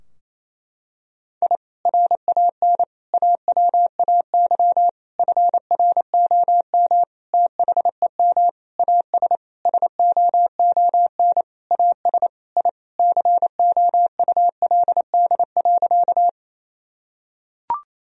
Morse code, heard as a keyed tone, about 28 wpm, 700 Hz. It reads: I RAN AWAY FROM THEM AS SOON AS I COULD.
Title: Sentences from Top 1000 Words - Review Part 3 - 28wpm